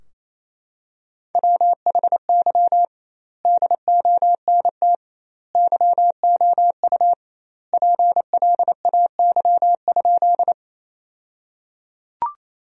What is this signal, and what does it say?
Morse code, heard as a keyed tone, about 28 wpm, 700 Hz. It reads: WHY DONT YOU PLAY?
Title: 2x Word-Spacing - Sentences from Top 300 Words - 28wpm